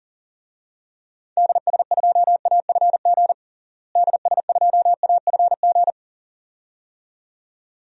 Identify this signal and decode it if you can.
Morse code, heard as a keyed tone, about 40 wpm, 700 Hz. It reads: DS2AFG DS2AFG
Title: DXCC-100 Call Signs - 6 Characters - Repeated 2x - 40wpm